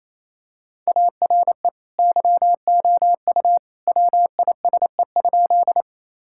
Morse code, heard as a keyed tone, about 28 wpm, 700 Hz. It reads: ARE YOU WISE?
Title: Sentences from Top 1600 Words 28wpm